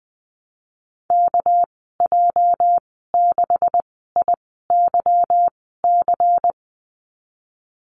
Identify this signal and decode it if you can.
Morse code, heard as a keyed tone, about 20 wpm, spaced as for 15 wpm, 700 Hz. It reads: KJ6IYC
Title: 219 Lesson 35g - Number 0 - Call Signs 20wpm